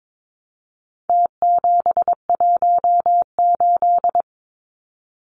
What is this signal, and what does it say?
Morse code, heard as a keyed tone, about 22 wpm, 700 Hz. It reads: T718
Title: Numbers - Contest Style - 4-Digit 22wpm